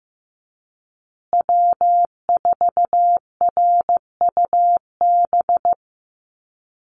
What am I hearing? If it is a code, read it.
Morse code, heard as a keyed tone, about 15 wpm, 700 Hz. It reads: W4RUB
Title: USA Call Signs Repeated 2x 15wpm